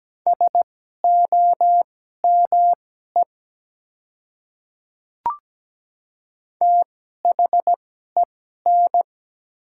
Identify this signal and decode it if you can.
Morse code, heard as a keyed tone, about 17 wpm, spaced as for 12 wpm, 700 Hz. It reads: SOME THEN
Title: 2x Character Spacing - Top 100 Words - 17wpm